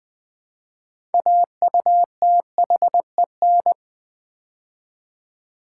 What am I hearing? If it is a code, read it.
Morse code, heard as a keyed tone, about 20 wpm, 700 Hz. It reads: AUTHEN
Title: Build A Word - 3 Letters At A Time - 9-Letter Words - 20wpm